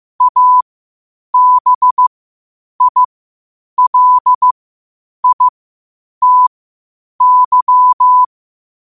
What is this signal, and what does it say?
Morse code, heard as a keyed tone, about 15 wpm, spaced as for 8 wpm, 1000 Hz. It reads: ABILITY
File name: qotdfiles / 8wpm